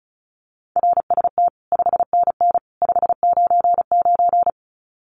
Morse code, heard as a keyed tone, about 34 wpm, 700 Hz. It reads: RST 5NN 599
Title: Common QSO Phrases 35wpm